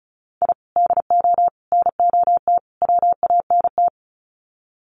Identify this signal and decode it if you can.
Morse code, heard as a keyed tone, about 34 wpm, 700 Hz. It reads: I DO NOT WANT
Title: Sets of 4 Words 35wpm